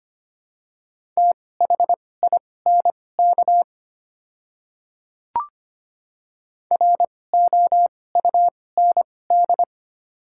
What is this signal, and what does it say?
Morse code, heard as a keyed tone, about 25 wpm, spaced as for 18 wpm, 700 Hz. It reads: THINK ROUND
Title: 2x Character Spacing - Top 200 Words - 25wpm